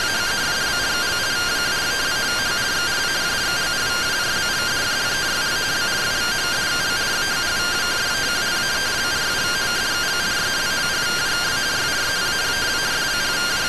Two-tone teleprinter signal, about 170 Hz shift.